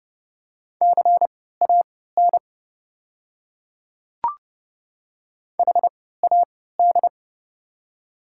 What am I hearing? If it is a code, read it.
Morse code, heard as a keyed tone, about 30 wpm, spaced as for 17 wpm, 700 Hz. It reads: CAN HAD
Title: 3x Character Spacing - Top 100 Words - 30wpm